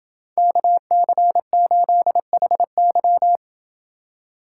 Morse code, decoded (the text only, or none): KC8HY